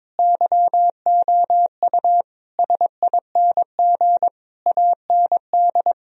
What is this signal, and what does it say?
Morse code, heard as a keyed tone, about 22 wpm, 700 Hz. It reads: YOU SING AND
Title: Sentences from Top 1000 Words - Review Part 4 - 22wpm